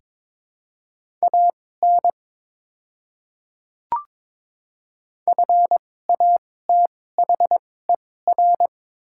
Morse code, heard as a keyed tone, about 22 wpm, spaced as for 16 wpm, 700 Hz. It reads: AN FATHER
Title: 2x Character Spacing - Top 200 Words - 22wpm